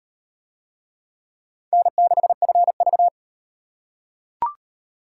Morse code, heard as a keyed tone, about 35 wpm, 700 Hz. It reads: N6FV